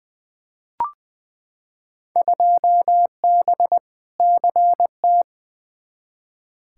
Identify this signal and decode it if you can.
Morse code, heard as a keyed tone, about 20 wpm, 700 Hz. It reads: 2B CT
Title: Field Day Exchanges Encore - 20wpm